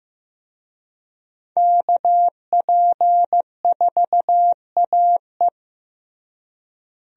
Morse code, heard as a keyed tone, about 15 wpm, 700 Hz. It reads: KP4AE
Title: USA Call Signs - 5 Characters - Repeated 2x - 15wpm